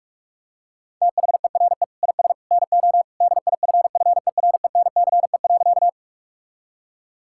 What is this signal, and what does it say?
Morse code, heard as a keyed tone, about 46 wpm, 700 Hz. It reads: THERE IS NO DIFFERENCE.
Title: Sentences from Top 1000 Words - Review Part 5 - 45wpm